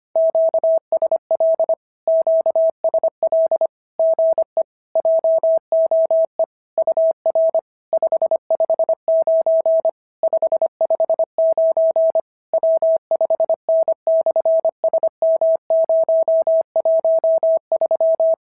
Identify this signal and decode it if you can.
Morse code, heard as a keyed tone, about 22 wpm, 650 Hz. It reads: QSL QSL GE JOE UR 559 559 W5N/SM013
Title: SOTA QSO Practice 25wpm